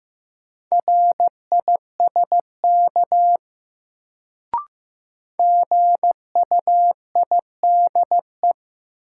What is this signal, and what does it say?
Morse code, heard as a keyed tone, about 15 wpm, 700 Hz. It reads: RISK GUIDE
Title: Sentences from Top 600 Words - New Word Drill - 15wpm